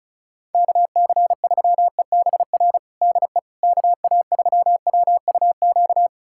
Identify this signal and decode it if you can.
Morse code, heard as a keyed tone, about 36 wpm, 700 Hz. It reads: KC3EBR DE KA3WUQ
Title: USA Callsign DE Callsign - 6 Characters - 35wpm